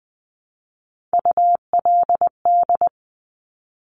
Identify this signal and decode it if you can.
Morse code, heard as a keyed tone, about 19 wpm, 700 Hz. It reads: ULD
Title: Top Three Letter Combinations in Words 20wpm